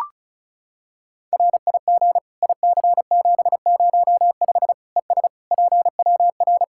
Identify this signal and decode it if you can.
Morse code, heard as a keyed tone, about 34 wpm, 700 Hz. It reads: RIG IC705 ES PWR
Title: Practice QSO - Exchange 2 - Info - 35wpm